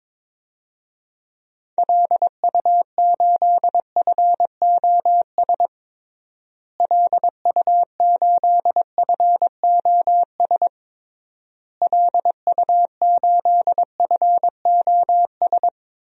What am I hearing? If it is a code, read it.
Morse code, heard as a keyed tone, about 22 wpm, 700 Hz. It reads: LU8FOS LU8FOS LU8FOS
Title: DXCC-100 Call Signs - 6 Characters - Repeated 3x - 22wpm